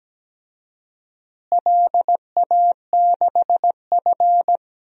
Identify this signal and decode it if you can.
Morse code, heard as a keyed tone, about 17 wpm, 700 Hz. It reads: LA6F
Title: DXCC-100 Call Signs - 4 Characters - Repeated 3x - 17wpm